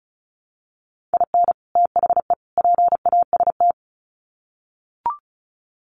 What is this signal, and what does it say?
Morse code, heard as a keyed tone, about 34 wpm, 700 Hz. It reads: IN THE PAST
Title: Sets of 3 Words 35wpm